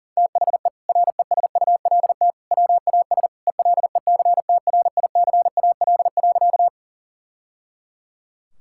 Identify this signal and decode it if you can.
Morse code, heard as a keyed tone, about 37 wpm, 700 Hz. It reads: THE RESULT WAS ELECTRICAL.
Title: Sentences from Top 1000 Words - Review Part 6 - 40wpm